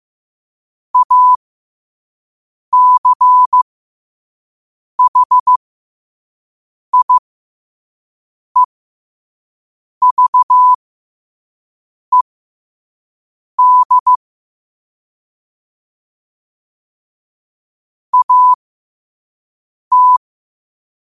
Morse code, heard as a keyed tone, about 15 wpm, spaced as for 5 wpm, 1000 Hz. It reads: ACHIEVED AT
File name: qotdfiles / 5wpm